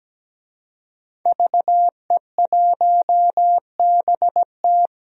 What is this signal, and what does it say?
Morse code, heard as a keyed tone, about 17 wpm, 700 Hz. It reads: VE1BT
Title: DXCC-100 Call Signs - 5 Characters - Repeated 2x - 17wpm